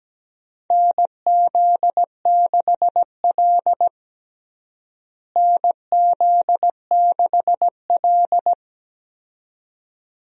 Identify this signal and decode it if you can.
Morse code, heard as a keyed tone, about 17 wpm, 700 Hz. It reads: NZ6L NZ6L
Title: USA Call Signs - 4 Characters - Repeated 2x - 17wpm